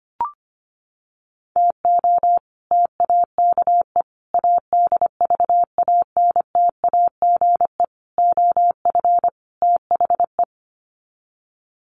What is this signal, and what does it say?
Morse code, heard as a keyed tone, about 25 wpm, 700 Hz. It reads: TO TAKE ADVANTAGE OF THE
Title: Sets of 5 Words 25wpm